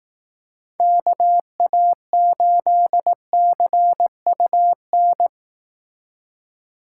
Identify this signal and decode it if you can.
Morse code, heard as a keyed tone, about 18 wpm, 700 Hz. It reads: KA8CUN
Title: Speed Racing - USA 2x3 Call Signs 15wpm